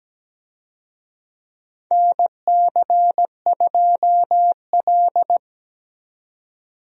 Morse code, heard as a keyed tone, about 17 wpm, 700 Hz. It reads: NC2L
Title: USA Call Signs - 4 Characters - 17wpm